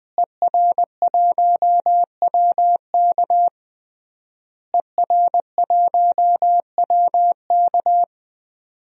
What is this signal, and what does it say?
Morse code, heard as a keyed tone, about 21 wpm, 700 Hz. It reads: ER1WK ER1WK
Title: DXCC-100 Call Signs - 5 Characters - Repeated 3x - 20wpm